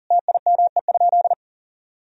Morse code, heard as a keyed tone, about 37 wpm, 700 Hz. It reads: TIME?
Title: Sentences from Top 200 Words and Abbreviations 40wpm